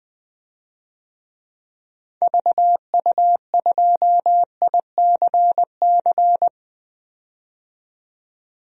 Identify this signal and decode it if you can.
Morse code, heard as a keyed tone, about 20 wpm, 700 Hz. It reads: VU2ICC
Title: DXCC-100 Call Signs - 6 Characters - 20wpm